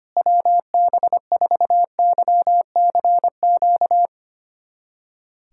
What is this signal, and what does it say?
Morse code, heard as a keyed tone, about 25 wpm, 700 Hz. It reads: WB4YCQ
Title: USA Callsign DE Callsign - 6 Characters - 25wpm